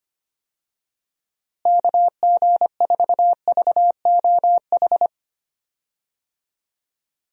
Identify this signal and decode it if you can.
Morse code, heard as a keyed tone, about 25 wpm, 700 Hz. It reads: KG4VOH
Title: USA Call Signs - 6 Characters - Repeated 3x - 25wpm